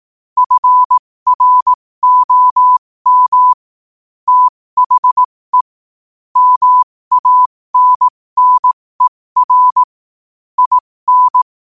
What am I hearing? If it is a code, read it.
Morse code, heard as a keyed tone, about 18 wpm, spaced as for 15 wpm, 1000 Hz. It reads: FROM THE MANNER IN
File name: qotdfiles / 15wpm